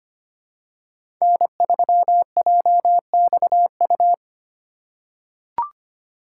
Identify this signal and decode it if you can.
Morse code, heard as a keyed tone, about 25 wpm, 700 Hz. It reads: N3JXU
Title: USA Call Signs - 3x Character Spacing - 25wpm